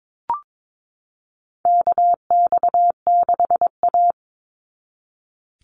Morse code, heard as a keyed tone, about 22 wpm, 700 Hz. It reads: KX6A
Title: NA SOTA Chaser Callsigns 22wpm